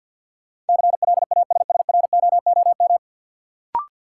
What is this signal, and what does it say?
Morse code, heard as a keyed tone, about 53 wpm, 700 Hz. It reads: CLASSROOM
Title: Sentences from Top 1500 Words - New Word Drill - 50wpm